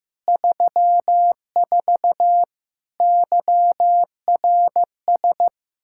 Morse code, heard as a keyed tone, about 15 wpm, 700 Hz. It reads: 34 YRS
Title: Practice QSO - Exchange 3 - More Info - 15wpm